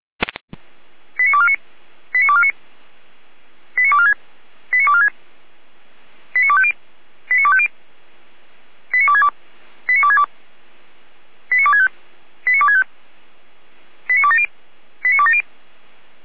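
Five-tone selective call (ZVEI), heard as quick runs of five tones, tones of about 70 ms. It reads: 89249 89249 89246 89246 89250 89250 89262 89262 89265 89265 89270 89270
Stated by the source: ZVEI five-tone selective call received by SDR.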